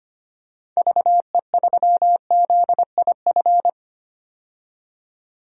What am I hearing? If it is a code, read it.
Morse code, heard as a keyed tone, about 25 wpm, 700 Hz. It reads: VE3ZIF
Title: DXCC-100 Call Signs - 6 Characters - 25wpm